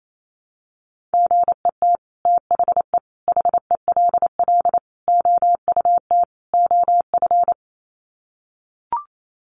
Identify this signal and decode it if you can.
Morse code, heard as a keyed tone, about 28 wpm, 700 Hz. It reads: GET THE HELL OUT OF
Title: Sets of 5 Words Encore2 28wpm